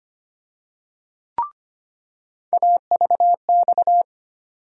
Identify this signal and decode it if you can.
Morse code, heard as a keyed tone, about 25 wpm, 700 Hz. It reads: AVX
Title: USA Call Sign Postfixes - 3 Letters 25wpm